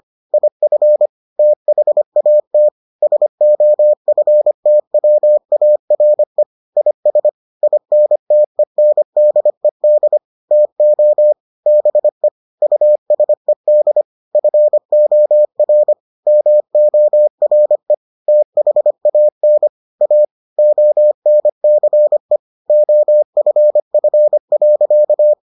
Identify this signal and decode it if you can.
Morse code, heard as a keyed tone, about 25 wpm, 600 Hz. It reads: IF THAT SOFTWARE IS INTENDED TO BE USED FOR MORE THAN A ONCE OFF.